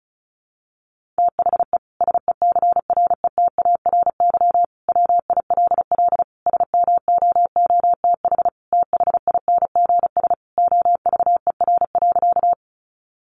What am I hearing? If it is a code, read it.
Morse code, heard as a keyed tone, about 34 wpm, 700 Hz. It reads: THE SECRETARY WILL SMOOTH THINGS OVER.
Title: Sentences from Top 1400 Words 35wpm